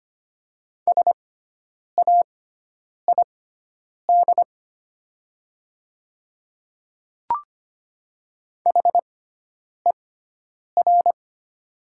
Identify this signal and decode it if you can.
Morse code, heard as a keyed tone, about 25 wpm, spaced as for 9 wpm, 700 Hz. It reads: SAID HER
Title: 6x Character Spacing - Top 300 Words - 25wpm